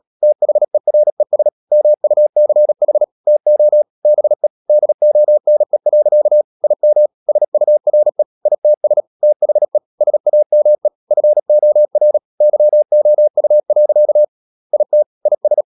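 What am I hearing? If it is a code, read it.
Morse code, heard as a keyed tone, about 37 wpm, 600 Hz. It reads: THERES MUCH TO BE DONE. IM SURE ITS THE SAME FOR YOU. IT IS
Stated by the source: Morse code practice at 35–39 wpm sending English text of Ken Follett's novel Never, from CW med Gnister.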